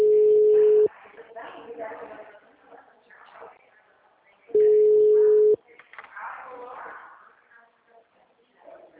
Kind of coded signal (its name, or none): telephone call progress tone